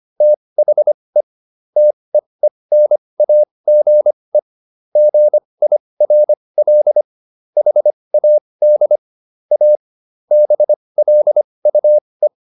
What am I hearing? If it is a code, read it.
Morse code, heard as a keyed tone, about 25 wpm, spaced as for 20 wpm, 600 Hz. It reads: THE TEENAGE GIRL HAD A BLUE